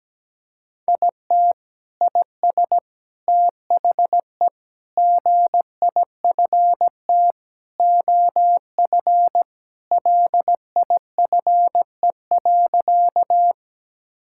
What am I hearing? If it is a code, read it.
Morse code, heard as a keyed tone, about 17 wpm, 700 Hz. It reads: IT IS THE GIFT OF LIFE.
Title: Sentences from Top 1000 Words - Review Part 6 - 17wpm